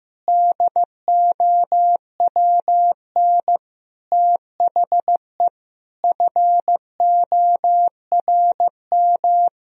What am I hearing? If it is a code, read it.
Morse code, heard as a keyed tone, about 16 wpm, 700 Hz. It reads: DOWN THE FORM
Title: Sentences from Top 1000 Words - Review Part 1 - 15wpm